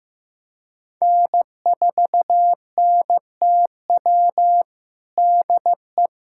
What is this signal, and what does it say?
Morse code, heard as a keyed tone, about 15 wpm, 700 Hz. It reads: N4NTW DE
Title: USA Callsign DE Callsign - 5 Characters - 15wpm